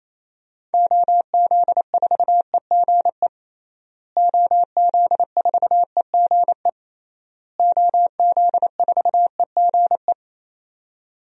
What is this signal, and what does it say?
Morse code, heard as a keyed tone, about 28 wpm, 700 Hz. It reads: OZ4EGE OZ4EGE OZ4EGE
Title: DXCC-100 Call Signs - 6 Characters - Repeated 3x - 28wpm